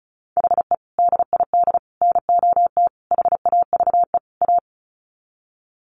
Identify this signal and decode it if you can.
Morse code, heard as a keyed tone, about 34 wpm, 700 Hz. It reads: HE DID NOT HAVE A
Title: Sets of 5 Words Encore 35wpm